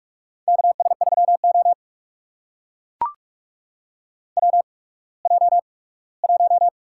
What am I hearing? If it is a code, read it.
Morse code, heard as a keyed tone, about 42 wpm, spaced as for 21 wpm, 700 Hz. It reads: KS3O WJ1
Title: USA Call Signs - 4 Characters - 8x Character Spacing - 45wpm